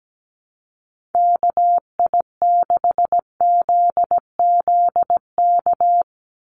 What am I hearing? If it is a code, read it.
Morse code, heard as a keyed tone, about 17 wpm, 700 Hz. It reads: KI6ZZK
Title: USA Call Signs 17wpm